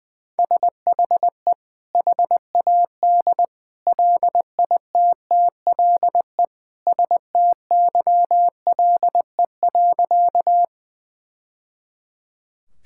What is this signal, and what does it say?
Morse code, heard as a keyed tone, about 20 wpm, 700 Hz. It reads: SHE HAD LITTLE STYLE.
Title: Sentences from Top 1000 Words - Review Part 5 - 20wpm